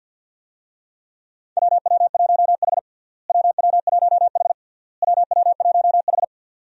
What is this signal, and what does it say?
Morse code, heard as a keyed tone, about 47 wpm, 700 Hz. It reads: WW1H WW1H WW1H
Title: USA Call Signs - 4 Characters - Repeated 3x - 50wpm